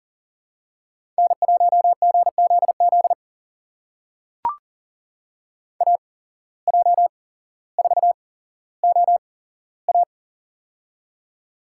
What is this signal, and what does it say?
Morse code, heard as a keyed tone, about 40 wpm, spaced as for 17 wpm, 700 Hz. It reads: N1GZZ AJ4OA
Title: USA Call Signs - 8x Character Spacing - 40wpm